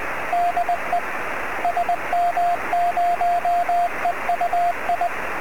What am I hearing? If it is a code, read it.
Morse code, heard as a keyed tone, about 20 wpm, 700 Hz, with radio noise. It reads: DE SM0EUI
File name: sm7lcb / 02-sm0eui 23cm